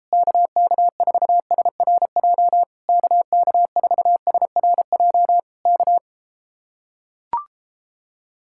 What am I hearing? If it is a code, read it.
Morse code, heard as a keyed tone, about 33 wpm, 700 Hz. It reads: KK4SRJ KK4SRJ K